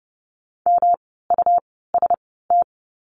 Morse code, heard as a keyed tone, about 30 wpm, spaced as for 17 wpm, 700 Hz. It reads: MUST
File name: course / 30:10wpm / 237 Lesson 38f - Letter q - Words 30wpm